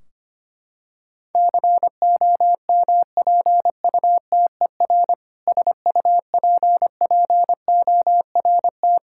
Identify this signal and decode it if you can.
Morse code, heard as a keyed tone, about 25 wpm, 700 Hz. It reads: COMPUTER SUPPORT
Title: Practice QSO - Exchange 3 - More Info - 25wpm